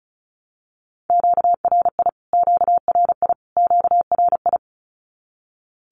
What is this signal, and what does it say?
Morse code, heard as a keyed tone, about 34 wpm, 700 Hz. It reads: QRI QRI QRI
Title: Q-Codes Repeated 3x 35wpm